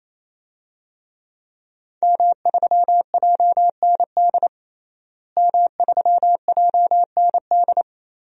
Morse code, heard as a keyed tone, about 28 wpm, 700 Hz. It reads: M3JND M3JND
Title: DXCC-100 Call Signs - 5 Characters - Repeated 3x - 28wpm